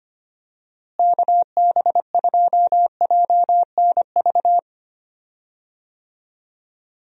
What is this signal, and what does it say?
Morse code, heard as a keyed tone, about 25 wpm, 700 Hz. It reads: KB2JNV
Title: USA Call Signs - 6 Characters - Repeated 3x - 25wpm